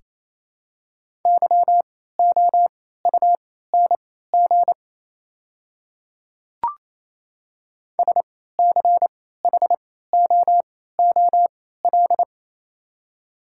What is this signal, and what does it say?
Morse code, heard as a keyed tone, about 28 wpm, spaced as for 16 wpm, 700 Hz. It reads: YOUNG SCHOOL
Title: 3x Character Spacing - Top 500 Words - 28wpm